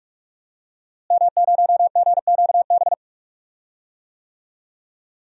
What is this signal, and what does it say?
Morse code, heard as a keyed tone, about 44 wpm, 700 Hz. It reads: M0GQB
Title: DXCC-100 Call Signs - 5 Characters - Repeated 3x - 45wpm